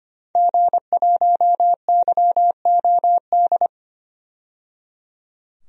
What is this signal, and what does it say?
Morse code, heard as a keyed tone, about 25 wpm, 700 Hz. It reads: G1YOD
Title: DXCC-100 Call Signs - 5 Characters - 25wpm